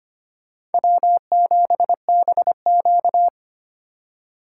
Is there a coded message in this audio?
Yes: Morse code from 0.7 to 3.3 s, heard as a keyed tone, about 33 wpm, 700 Hz. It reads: W7BQ